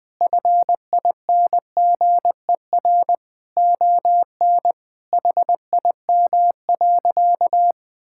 Morse code, heard as a keyed tone, about 21 wpm, 700 Hz. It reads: FINGER ON HIM.
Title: Sentences from Top 1600 Words 20wpm